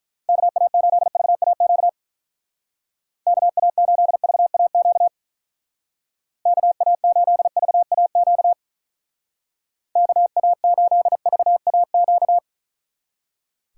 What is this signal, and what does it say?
Morse code, heard as a keyed tone, about 44 wpm, 700 Hz. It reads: KA8VAQ KA8VAQ KA8VAQ KA8VAQ